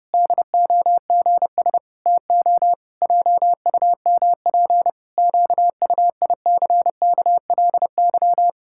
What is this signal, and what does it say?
Morse code, heard as a keyed tone, about 32 wpm, 700 Hz. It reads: DOGS TO JUMP QUICKLY
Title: Pangram Challenge v2 - 30wpm